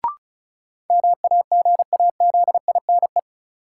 Morse code, heard as a keyed tone, about 34 wpm, 700 Hz. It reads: MAGAZINE
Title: Sentences from Top 1000 Words - Review Part 7 - New Word Drill - 35wpm